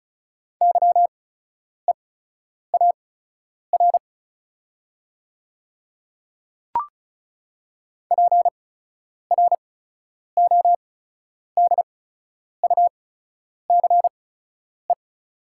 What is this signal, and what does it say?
Morse code, heard as a keyed tone, about 36 wpm, spaced as for 10 wpm, 690 Hz. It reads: YEAR PRODUCE
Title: 8x Character Spacing - Top 500 Words - 35wpm